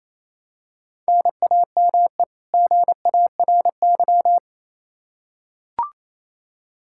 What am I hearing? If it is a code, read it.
Morse code, heard as a keyed tone, about 28 wpm, 700 Hz. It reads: NAME GARY
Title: Practice QSO - Exchange 1 - Beginning - 28wpm